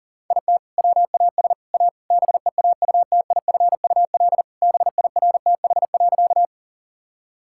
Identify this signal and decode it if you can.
Morse code, heard as a keyed tone, about 40 wpm, 700 Hz. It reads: IT WAS A BEAUTIFUL BIRTH.
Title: Sentences from Top 1000 Words - Review Part 5 - 40wpm